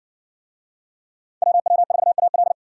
Morse code, heard as a keyed tone, about 60 wpm, 700 Hz. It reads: WW4AL